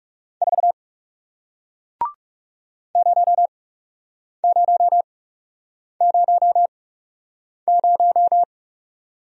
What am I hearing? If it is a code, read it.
Morse code, heard as a keyed tone, about 38 wpm, spaced as for 8 wpm, 700 Hz. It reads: 4 0000